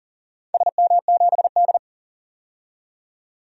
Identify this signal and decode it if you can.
Morse code, heard as a keyed tone, about 40 wpm, 700 Hz. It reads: SM7D
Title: DXCC-100 Call Signs - 4 Characters - 40wpm